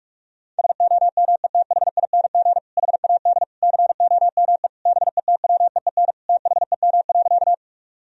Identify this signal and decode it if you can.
Morse code, heard as a keyed tone, about 46 wpm, 700 Hz. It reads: SOMETHING HAD COME BETWEEN THEM.
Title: Sentences from Top 1000 Words - Review Part 1 - 45wpm